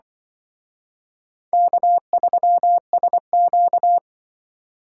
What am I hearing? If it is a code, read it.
Morse code, heard as a keyed tone, about 23 wpm, 700 Hz. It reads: K3SQ
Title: Speed Racing - USA 1x2 Call Signs 20wpm